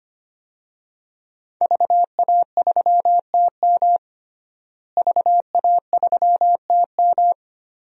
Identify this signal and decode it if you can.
Morse code, heard as a keyed tone, about 25 wpm, 700 Hz. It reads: VA3TM VA3TM
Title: DXCC-100 Call Signs - 5 Characters - Repeated 2x - 25wpm